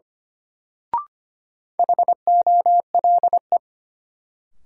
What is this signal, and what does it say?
Morse code, heard as a keyed tone, about 25 wpm, 700 Hz. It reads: HOLE
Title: Sentences from Top 1600 Words - New Word Drill - 25wpm